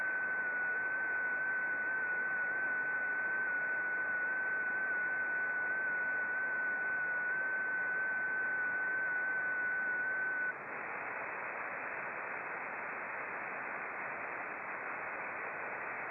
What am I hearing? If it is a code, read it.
FT8.
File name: contests / sept-vhf-k5tr- -0300